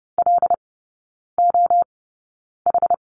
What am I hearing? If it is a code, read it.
Morse code, heard as a keyed tone, about 30 wpm, spaced as for 9 wpm, 700 Hz. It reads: LOH
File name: course / 30:10wpm / 087 Lesson 15e - Number 5 - Three Characters 30wpm